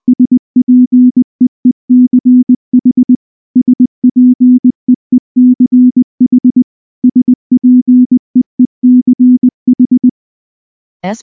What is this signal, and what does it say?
Morse code, heard as a keyed tone, about 19 wpm, 250 Hz. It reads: SPEECH SPEECH SPEECH
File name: morse / morse-training-20WPM-7